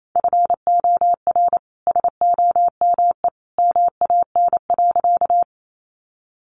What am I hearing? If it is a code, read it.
Morse code, heard as a keyed tone, about 28 wpm, 700 Hz. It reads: FOR SOME MAN.
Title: Sentences from Top 200 Words 28wpm